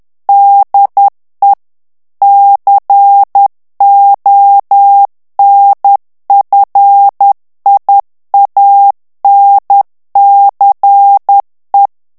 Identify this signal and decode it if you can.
Morse code, heard as a keyed tone, about 10 wpm, 800 Hz. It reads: DE CONFIANCE
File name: FRANCAIS TEXTE CLAIR 2 VITESSE 10 MOTS MINUTES